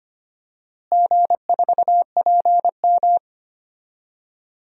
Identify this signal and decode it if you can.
Morse code, heard as a keyed tone, about 25 wpm, 700 Hz. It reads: G4PM
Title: DXCC-100 Call Signs - 4 Characters - Repeated 3x - 25wpm